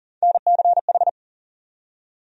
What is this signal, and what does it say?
Morse code, heard as a keyed tone, about 40 wpm, 700 Hz. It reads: NCH
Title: USA Call Sign Postfixes - 3 Letters 40wpm